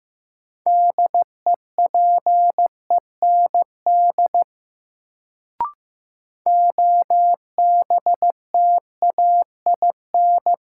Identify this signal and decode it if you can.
Morse code, heard as a keyed tone, about 15 wpm, 700 Hz. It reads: DEPEND OBTAIN
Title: Sentences from Top 1300 Words - New Word Drill Alt - 15wpm